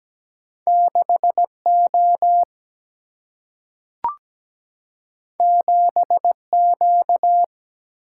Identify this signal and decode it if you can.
Morse code, heard as a keyed tone, about 17 wpm, 700 Hz. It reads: 6O 7Q
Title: Single-Letter Single-Number 17wpm